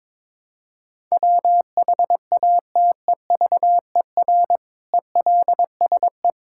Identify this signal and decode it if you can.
Morse code, heard as a keyed tone, about 22 wpm, 700 Hz. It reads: WHATEVER ELSE
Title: Sentences from Top 1600 Words 22wpm